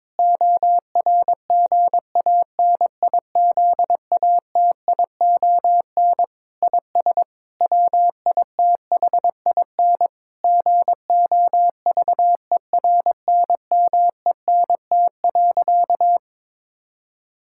Morse code, heard as a keyed tone, about 22 wpm, 700 Hz. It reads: ORGANIZATION IS WITHIN GOVERNMENT.
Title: Sentences from Top 1000 Words - Review Part 5 - 22wpm